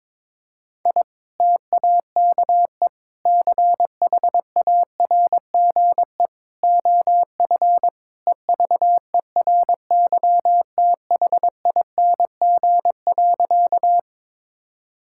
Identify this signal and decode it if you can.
Morse code, heard as a keyed tone, about 22 wpm, 700 Hz. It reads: I TAKE CHARGE OF EVERYTHING.
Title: Sentences from Top 1000 Words - Review Part 6 - 22wpm